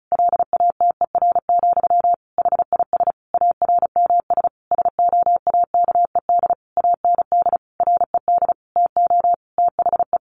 Morse code, heard as a keyed tone, about 34 wpm, 700 Hz. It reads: LATER, HIS ARMS SOAKED AND RED TO THE
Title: A Sound Of Thunder 35wpm